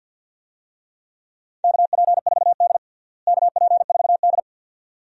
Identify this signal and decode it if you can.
Morse code, heard as a keyed tone, about 50 wpm, 700 Hz. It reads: KP4D KP4D
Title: DXCC-100 Call Signs - 4 Characters - Repeated 2x - 50wpm